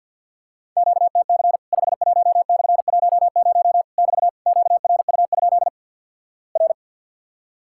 Morse code, heard as a keyed tone, about 50 wpm, 700 Hz. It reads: QTC 51/10 <BT> QRU? R
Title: Worked All Europe DX Contest - QTC Exchange with Repeats - 50wpm